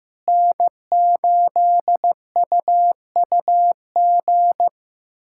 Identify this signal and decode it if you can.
Morse code, heard as a keyed tone, about 15 wpm, 700 Hz. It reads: N8UUG
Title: USA Callsign DE Callsign - 5 Characters - 15wpm